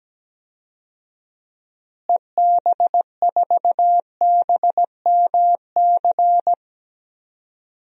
Morse code, heard as a keyed tone, about 17 wpm, 700 Hz. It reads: EB4BMC
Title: DXCC-100 Call Signs - 6 Characters - Repeated 3x - 17wpm